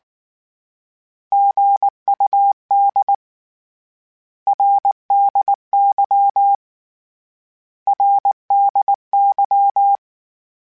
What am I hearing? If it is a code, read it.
Morse code, heard as a keyed tone, about 19 wpm, 800 Hz. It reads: GUD RDY RDY
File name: Abreviations / Abreviations mp3s / Abreviations@18